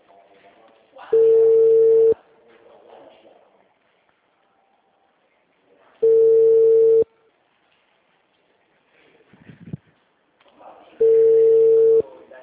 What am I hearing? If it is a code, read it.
Ringback tone (China, 450 Hz), with speech.